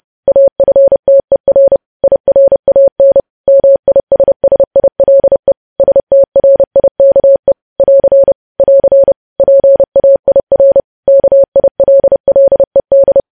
Morse code, heard as a keyed tone, about 30 wpm, 550 Hz. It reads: AFTER IRAN MISSILE STRIKE <AR> <AR> PAIR KILLED
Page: News Headlines in Morse Code at 30 WPM